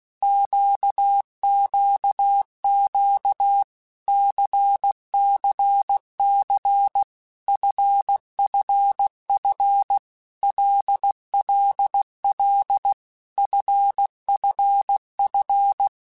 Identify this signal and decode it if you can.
Morse code, heard as a keyed tone, about 16 wpm, 800 Hz. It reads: QQQ CCC FFF LLL FFF
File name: Basic Letter Groups By7Groups / LFCYQ / LFCYQ@15